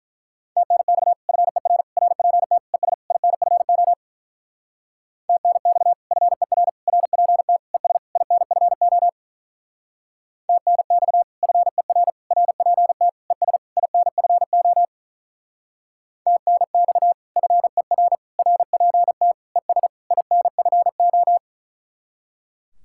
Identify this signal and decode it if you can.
Morse code, heard as a keyed tone, about 45 wpm, 700 Hz. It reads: TNX FER RPT ES INFO TNX FER RPT ES INFO TNX FER RPT ES INFO TNX FER RPT ES INFO